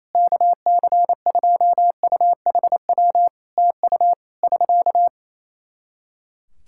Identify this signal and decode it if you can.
Morse code, heard as a keyed tone, about 28 wpm, 700 Hz. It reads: KC2UHW TU <SK>
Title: K1USN SST QSO Practice 28wpm